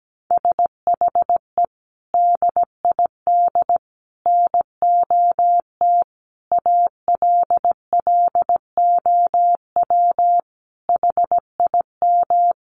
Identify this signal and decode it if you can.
Morse code, heard as a keyed tone, about 17 wpm, 700 Hz. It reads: SHE DID NOT ALLOW HIM
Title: Sentences from Top 1200 Words 17wpm